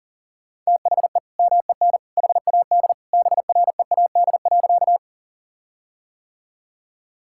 Morse code, heard as a keyed tone, about 40 wpm, 700 Hz. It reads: THE MEN HAD BREAD.
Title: Sentences from Top 1600 Words 40wpm